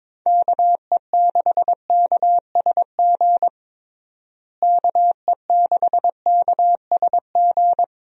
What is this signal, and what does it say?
Morse code, heard as a keyed tone, about 22 wpm, 700 Hz. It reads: KE6KSG KE6KSG
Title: USA Call Signs - 6 Characters - Repeated 2x - 22wpm